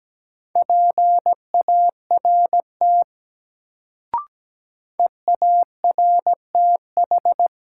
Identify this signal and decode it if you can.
Morse code, heard as a keyed tone, about 17 wpm, 700 Hz. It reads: PART EARTH
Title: Sentences from Top 200 Words - New Word Drill - 17wpm